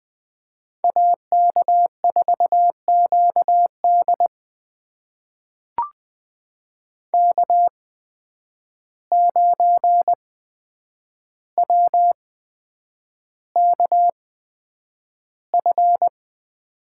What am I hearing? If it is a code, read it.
Morse code, heard as a keyed tone, about 20 wpm, spaced as for 9 wpm, 700 Hz. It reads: AK4QD K9WKF
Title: USA Call Signs - 5 Characters - 8x Character Spacing - 20wpm